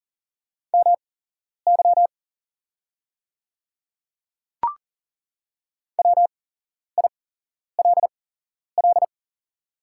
Morse code, heard as a keyed tone, about 40 wpm, spaced as for 11 wpm, 690 Hz. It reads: MY WILL
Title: 8x Character Spacing - Top 100 Words - 40wpm